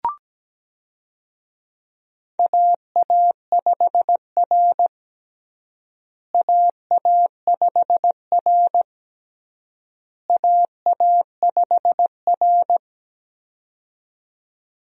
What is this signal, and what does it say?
Morse code, heard as a keyed tone, about 17 wpm, 700 Hz. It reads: AA5R AA5R AA5R
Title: USA Call Signs - 4 Characters - Repeated 3x - 17wpm